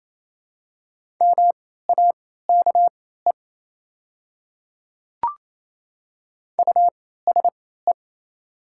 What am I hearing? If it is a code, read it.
Morse code, heard as a keyed tone, about 28 wpm, spaced as for 16 wpm, 700 Hz. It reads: MAKE USE
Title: 3x Character Spacing - Top 200 Words - 28wpm